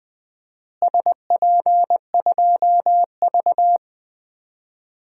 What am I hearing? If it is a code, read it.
Morse code, heard as a keyed tone, about 21 wpm, 700 Hz. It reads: SP2V
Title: DXCC-100 Call Signs - 4 Characters - Repeated 2x - 20wpm